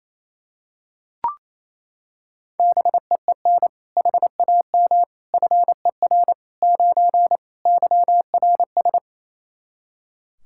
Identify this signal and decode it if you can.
Morse code, heard as a keyed tone, about 28 wpm, 700 Hz. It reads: BEEN HAM FER 9 YRS
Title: Practice QSO - Exchange 3 - More Info - 28wpm